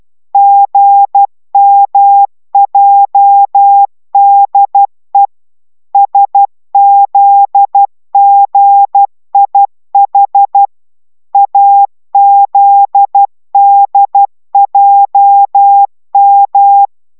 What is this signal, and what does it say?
Morse code, heard as a keyed tone, about 12 wpm, 800 Hz. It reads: GMJDE SZGIH AZDJM